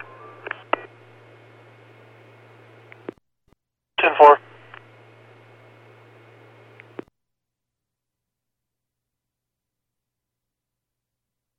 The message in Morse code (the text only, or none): none